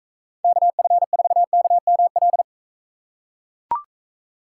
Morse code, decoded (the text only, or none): KF4KML